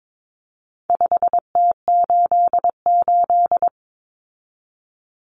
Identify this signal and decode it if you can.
Morse code, heard as a keyed tone, about 22 wpm, 700 Hz. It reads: 5T88
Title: Common Cut Numbers - 4-Digit 22wpm